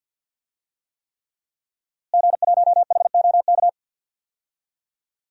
Morse code, heard as a keyed tone, about 52 wpm, 700 Hz. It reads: G1HOK
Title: DXCC-100 Call Signs - 5 Characters - Repeated 3x - 50wpm